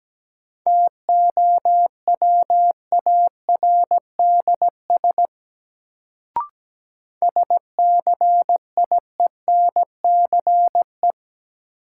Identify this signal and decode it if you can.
Morse code, heard as a keyed tone, about 17 wpm, 700 Hz. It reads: TOWARDS SCIENCE
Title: Sentences from Top 1000 Words - Review Part 2 - New Word Drill - 17wpm